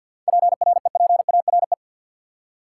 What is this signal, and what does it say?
Morse code, heard as a keyed tone, about 52 wpm, 700 Hz. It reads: PREPARE
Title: Seven Letter Words - 50wpm